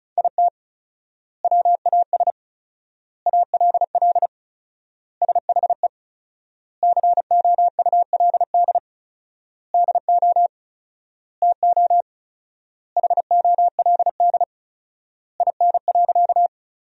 Morse code, heard as a keyed tone, about 34 wpm, 700 Hz. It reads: IT WAS ALL SHE COULD DO TO HOLD IN.
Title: 4x Word-Spacing - Sentences from Top 400 Words - 35wpm